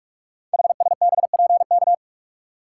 Morse code, heard as a keyed tone, about 44 wpm, 700 Hz. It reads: HSBPK
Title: Code Groups - 5 Letters - 45wpm